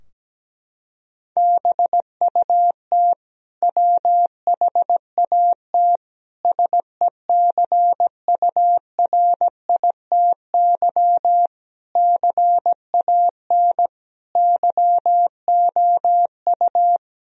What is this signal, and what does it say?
Morse code, heard as a keyed tone, about 17 wpm, 700 Hz. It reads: BUT WHAT SECURITY CAN YOU
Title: Sentences from Top 1000 Words - Review Part 4 - 17wpm